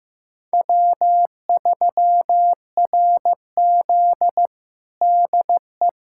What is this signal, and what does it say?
Morse code, heard as a keyed tone, about 15 wpm, 700 Hz. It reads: W3RZ DE
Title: USA Callsign DE Callsign - 4 Characters - 15wpm